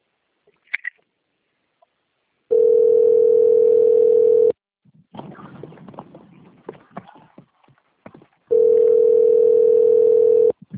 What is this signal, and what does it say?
Ringback tone (North America), 440 + 480 Hz.